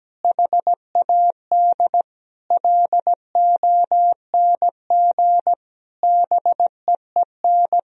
Morse code, heard as a keyed tone, about 17 wpm, 700 Hz. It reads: HAD LONG BEEN
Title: Sentences from Top 1000 Words - Review Part 3 - 17wpm